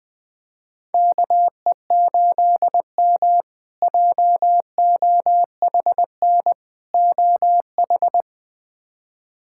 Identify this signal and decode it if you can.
Morse code, heard as a keyed tone, about 20 wpm, 700 Hz. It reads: KE8M JOHN OH
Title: K1USN SST QSO Practice - 20wpm